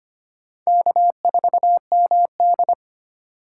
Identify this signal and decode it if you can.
Morse code, heard as a keyed tone, about 25 wpm, 700 Hz. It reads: K4MD
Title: USA Call Signs - 4x Character Spacing - 25wpm